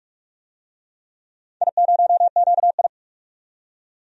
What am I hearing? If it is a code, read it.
Morse code, heard as a keyed tone, about 46 wpm, 700 Hz. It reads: I0QI
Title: DXCC-100 Call Signs - 4 Characters - 45wpm